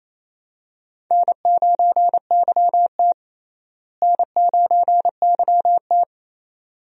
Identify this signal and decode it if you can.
Morse code, heard as a keyed tone, about 28 wpm, 700 Hz. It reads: N9YT N9YT
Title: USA Call Signs - 4 Characters - Repeated 2x - 28wpm